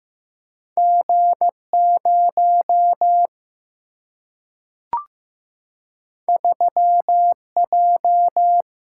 Morse code, heard as a keyed tone, about 15 wpm, 700 Hz. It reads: G 0 3 J
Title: Single-Letter Single-Number 15wpm